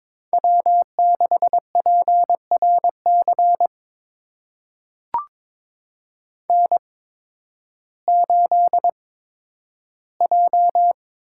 Morse code, heard as a keyed tone, about 22 wpm, spaced as for 12 wpm, 700 Hz. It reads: W6PRC N8J
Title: USA Call Signs - 5 Characters - 8x Character Spacing - 22wpm